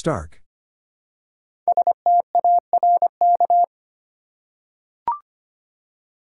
Morse code, heard as a keyed tone, about 25 wpm, 700 Hz. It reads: STARK
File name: 5 Letter Words - First 3 Letters Given - 25wpm